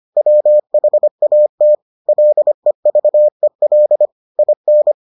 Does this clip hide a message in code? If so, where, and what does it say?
Yes: Morse code from 0.2 to 4.9 s, heard as a keyed tone, about 25 wpm, 600 Hz. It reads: WHAT LEVEL IN